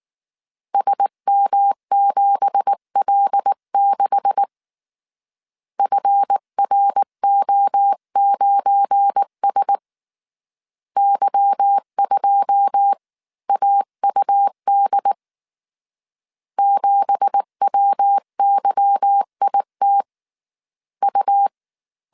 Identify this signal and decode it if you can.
Morse code, heard as a keyed tone, about 19 wpm, 800 Hz. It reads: SM7L6 FRO9S Y2AUD 7WYIT U
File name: CW / DET / 1 9 1500C S C 18wpm 2539